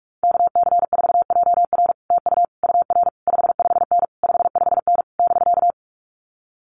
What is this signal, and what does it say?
Morse code, heard as a keyed tone, about 46 wpm, 700 Hz. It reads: KC4JR TU UR 55N 55N <BK>
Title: Common QSO Phrases Encore 45wpm